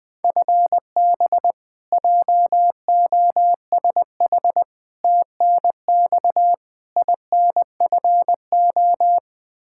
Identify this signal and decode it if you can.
Morse code, heard as a keyed tone, about 20 wpm, 700 Hz. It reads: FB JOSH TNX INFO
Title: Practice QSO - Exchange 3 - More Info - 20wpm